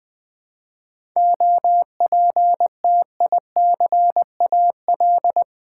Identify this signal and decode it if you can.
Morse code, heard as a keyed tone, about 20 wpm, 700 Hz. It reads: OPTICAL
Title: 7 Letter Words - First 3 Letters Given - 20wpm